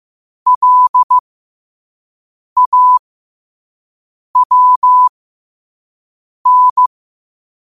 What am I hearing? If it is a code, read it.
Morse code, heard as a keyed tone, about 15 wpm, spaced as for 5 wpm, 1000 Hz. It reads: LAWN